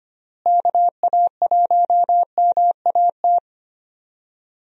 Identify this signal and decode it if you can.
Morse code, heard as a keyed tone, about 25 wpm, 700 Hz. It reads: KA1MAT
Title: USA Call Signs - 6 Characters - Repeated 3x - 25wpm